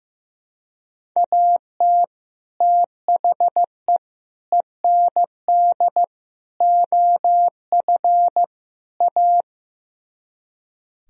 Morse code, heard as a keyed tone, about 15 wpm, 700 Hz. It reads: AT THE END OF A
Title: Speed-Racing - Sets of 5 Words 15wpm